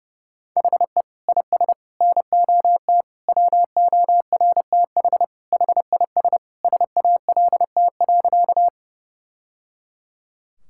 Morse code, heard as a keyed tone, about 32 wpm, 700 Hz. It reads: HE IS NOT WORTH HIS SALT.
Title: Sentences from Top 1000 Words - Review Part 5 - 30wpm